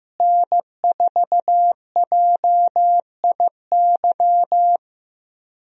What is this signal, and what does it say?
Morse code, heard as a keyed tone, about 15 wpm, 700 Hz. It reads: N4JIY
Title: USA Call Signs - 5 Characters - Repeated 3x - 15wpm